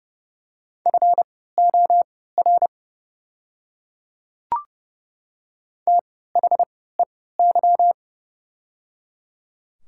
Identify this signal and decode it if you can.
Morse code, heard as a keyed tone, about 30 wpm, spaced as for 17 wpm, 700 Hz. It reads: FOR THEY
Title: 3x Character Spacing - Top 100 Words - 30wpm